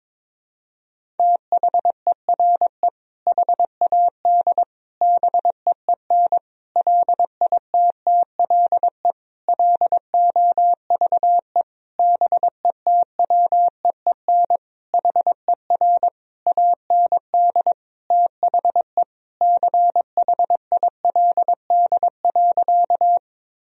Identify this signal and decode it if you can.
Morse code, heard as a keyed tone, about 22 wpm, 700 Hz. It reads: THERE HAD BEEN LITTLE LOVE BETWEEN HER AND THE CHILD.
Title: Sentences from Top 1000 Words - Review Part 1 - 22wpm